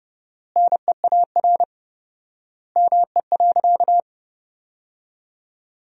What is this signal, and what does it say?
Morse code, heard as a keyed tone, about 30 wpm, 700 Hz. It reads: NEAR ME.
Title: 4x Word-Spacing - Sentences from Top 300 Words - 30wpm